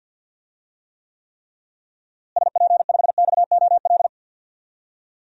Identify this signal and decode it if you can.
Morse code, heard as a keyed tone, about 50 wpm, 700 Hz. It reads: SP5XOL
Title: DXCC-100 Call Signs - 6 Characters - Repeated 3x - 50wpm